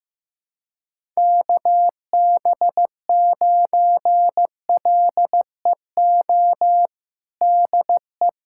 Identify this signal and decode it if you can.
Morse code, heard as a keyed tone, about 15 wpm, 700 Hz. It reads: KB9LEO DE
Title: USA Callsign DE Callsign - 6 Characters - 15wpm